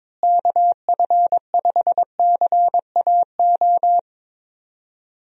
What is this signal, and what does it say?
Morse code, heard as a keyed tone, about 21 wpm, 700 Hz. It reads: KF5CAO
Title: USA Call Signs - 6 Characters - Repeated 3x - 22wpm